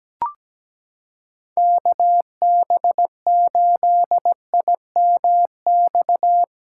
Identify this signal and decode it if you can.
Morse code, heard as a keyed tone, about 17 wpm, 700 Hz. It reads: KB8IMX
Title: USA Callsign DE Callsign 17wpm